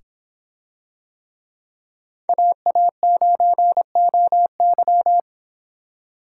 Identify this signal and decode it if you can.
Morse code, heard as a keyed tone, about 24 wpm, 700 Hz. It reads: AA9OY